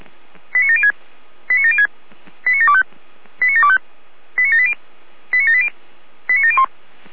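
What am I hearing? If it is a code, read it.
Five-tone selective call (ZVEI), heard as quick runs of five tones, tones of about 80 ms. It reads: 89706 89706 89725 89725 89780 89780 89701
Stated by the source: Five-tone ZVEI selective call received on an SDR, from the Feuerwehr Orlamünde fire brigade server.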